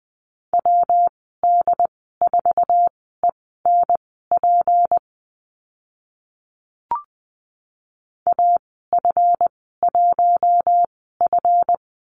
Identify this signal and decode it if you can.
Morse code, heard as a keyed tone, about 20 wpm, spaced as for 15 wpm, 700 Hz. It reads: WD4ENP AF1F
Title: 154 Lesson 25g - Letter p - Call Signs 20wpm